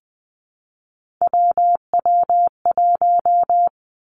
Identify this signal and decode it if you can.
Morse code, heard as a keyed tone, about 19 wpm, 700 Hz. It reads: WW1
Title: Three Challenging Letter-Number Combinations 20wpm